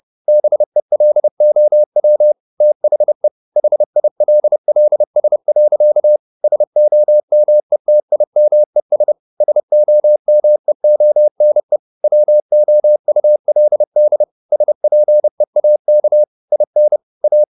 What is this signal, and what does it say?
Morse code, heard as a keyed tone, about 30 wpm, 600 Hz. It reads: BELOW THE HILLS. SOMETIMES SOMEONE WOULD SPEAK IN A
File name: the old man and the sea / the old man and the sea 1 30wpm